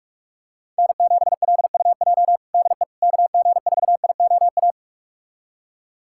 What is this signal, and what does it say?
Morse code, heard as a keyed tone, about 42 wpm, 700 Hz. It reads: N7LUJ DE KG4IOA
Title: USA Callsign DE Callsign 45wpm